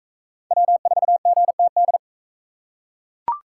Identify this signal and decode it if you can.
Morse code, heard as a keyed tone, about 42 wpm, 700 Hz. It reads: W4GTD